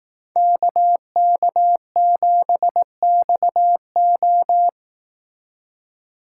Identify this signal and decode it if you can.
Morse code, heard as a keyed tone, about 18 wpm, 700 Hz. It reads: KK7XO
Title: Speed Racing - USA 2x2 Call Signs 15wpm